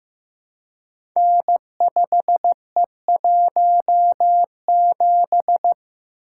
Morse code, heard as a keyed tone, about 15 wpm, 700 Hz. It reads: N5E17
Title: Code Groups - 5 Letter:Numbers - 15wpm